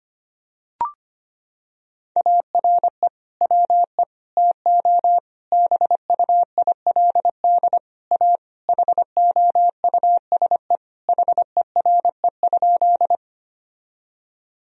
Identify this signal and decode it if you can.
Morse code, heard as a keyed tone, about 25 wpm, 700 Hz. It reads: ARE WE TO BUILD A HOUSE HERE?
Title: Sentences from Top 1000 Words - Review Part 1 - 25wpm